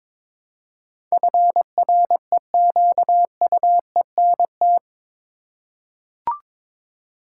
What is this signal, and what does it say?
Morse code, heard as a keyed tone, about 22 wpm, 700 Hz. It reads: FREQUENT
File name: practice / 22wpm / Eight Letter Words - 22wpm